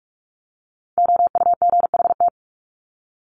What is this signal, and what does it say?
Morse code, heard as a keyed tone, about 44 wpm, 700 Hz. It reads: OUGHT
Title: Top 1751-2000 Words 45wpm